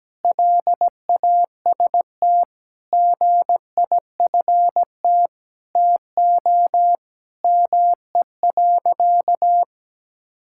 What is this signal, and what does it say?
Morse code, heard as a keyed tone, about 17 wpm, 700 Hz. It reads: LAST GIFT TO ME.
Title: Sentences from Top 1000 Words - Review Part 6 - 17wpm